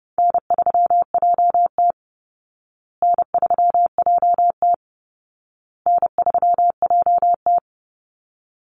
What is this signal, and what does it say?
Morse code, heard as a keyed tone, about 30 wpm, 700 Hz. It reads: N3JT N3JT N3JT
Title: Call Signs Repeated 2x - 30wpm